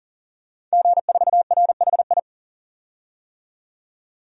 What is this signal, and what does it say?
Morse code, heard as a keyed tone, about 40 wpm, 700 Hz. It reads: G4RHI
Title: DXCC-100 Call Signs - 5 Characters - 40wpm